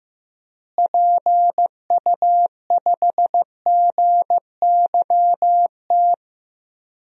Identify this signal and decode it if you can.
Morse code, heard as a keyed tone, about 15 wpm, 700 Hz. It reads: PU5GYT
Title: DXCC-100 Call Signs - 6 Characters - Repeated 2x - 15wpm